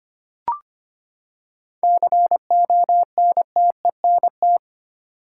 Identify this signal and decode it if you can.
Morse code, heard as a keyed tone, about 25 wpm, 700 Hz. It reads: CONTENT
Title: Seven Letter Words - 25wpm